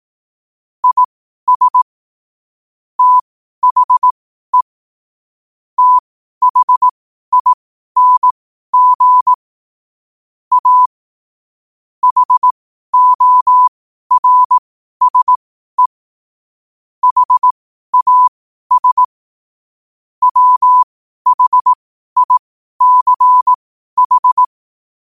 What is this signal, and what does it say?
Morse code, heard as a keyed tone, about 18 wpm, spaced as for 12 wpm, 1000 Hz. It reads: IS THE THING A HORSE HAS WHICH
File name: qotdfiles / 12wpm